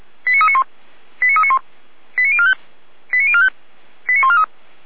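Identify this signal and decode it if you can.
Five-tone selective call (ZVEI), heard as quick runs of five tones, tones of about 70 ms. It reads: 89391 89391 89045 89045 89143
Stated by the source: ZVEI five-tone call received by SDR.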